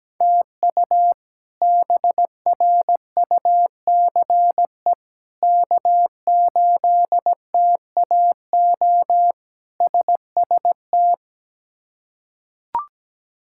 Morse code, heard as a keyed tone, about 17 wpm, 700 Hz. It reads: TU BRUCE K8TAO SST
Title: K1USN SST QSO Practice - 17wpm